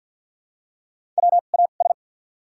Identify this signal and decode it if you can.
Morse code, heard as a keyed tone, about 53 wpm, spaced as for 37 wpm, 700 Hz. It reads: WAS
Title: 2x Character Spacing - Top 100 Words - 50wpm